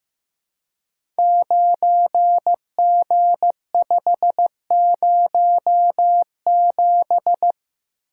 Morse code, heard as a keyed tone, about 15 wpm, 700 Hz. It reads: 9G507
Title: Code Groups - 5 Letter:Numbers - 15wpm